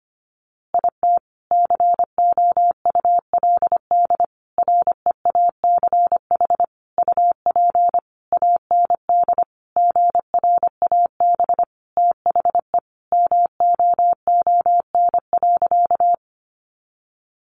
Morse code, heard as a keyed tone, about 25 wpm, 700 Hz. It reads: IT COULD REACH UP AND GRAB THE MOON.
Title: A Sound Of Thunder 25wpm